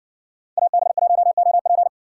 Morse code, heard as a keyed tone, about 60 wpm, 700 Hz. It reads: AB1OP